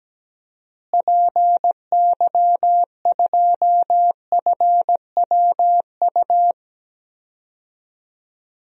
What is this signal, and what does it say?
Morse code, heard as a keyed tone, about 17 wpm, 700 Hz. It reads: PY2FWU
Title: DXCC-100 Call Signs - 6 Characters - Repeated 2x - 17wpm